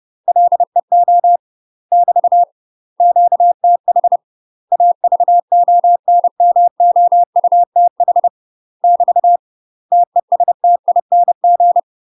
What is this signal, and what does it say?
Morse code, heard as a keyed tone, about 30 wpm, 700 Hz. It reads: LEO <BT> QTH AVONMOUTH <BT> TESTING
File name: UK MASTER QSOs TRAINING/30WPM